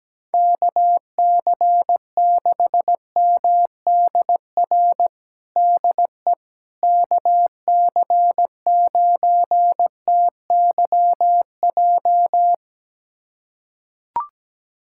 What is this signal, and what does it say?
Morse code, heard as a keyed tone, about 17 wpm, 700 Hz. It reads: KC6MDR DE KC9TYJ
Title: USA Callsign DE Callsign - 6 Characters - 17wpm